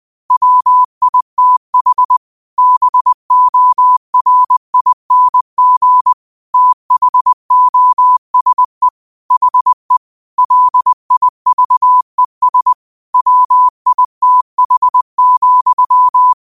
Morse code, heard as a keyed tone, about 20 wpm, 1000 Hz. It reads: WITH BORING THOSE HE LIVES WITH,
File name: qotdfiles / 20wpm